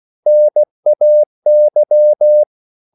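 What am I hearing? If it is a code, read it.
Morse code, heard as a keyed tone, about 16 wpm, 600 Hz. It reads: NAY
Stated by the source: CW practice of national capitals at 16 words per minute, from ON6ZQ.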